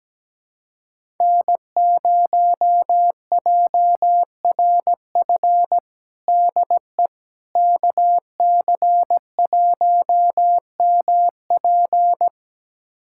Morse code, heard as a keyed tone, about 17 wpm, 700 Hz. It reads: N0JRF DE KC1MP
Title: USA Callsign DE Callsign - 5 Characters - 17wpm